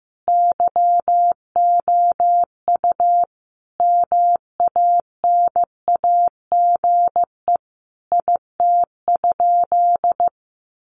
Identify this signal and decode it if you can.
Morse code, heard as a keyed tone, about 15 wpm, 700 Hz. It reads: YOU MANAGE IT?
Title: Sentences from Top 1400 Words 15wpm